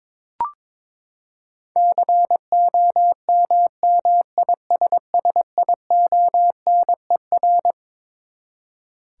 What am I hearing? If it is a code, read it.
Morse code, heard as a keyed tone, about 22 wpm, 700 Hz. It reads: COMMISSIONER
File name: Twelve Letter Words - 22wpm